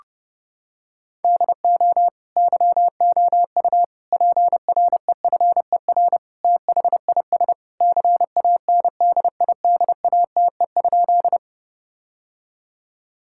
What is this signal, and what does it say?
Morse code, heard as a keyed tone, about 30 wpm, 700 Hz. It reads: DO YOU PREFER THIS CANDIDATE?
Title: Sentences from Top 1600 Words 30wpm